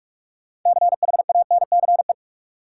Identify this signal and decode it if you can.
Morse code, heard as a keyed tone, about 44 wpm, 700 Hz. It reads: CHANCE
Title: Sentences from Top 800 Words - New Word Drill - 45wpm